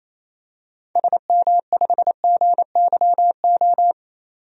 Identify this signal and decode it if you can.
Morse code, heard as a keyed tone, about 28 wpm, 700 Hz. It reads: SM5GYO
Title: DXCC-100 Call Signs - 6 Characters - 28wpm